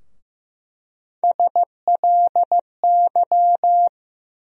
Morse code, heard as a keyed tone, about 15 wpm, 700 Hz. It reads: SLY
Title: USA Call Sign Postfixes - 3 Letters 15wpm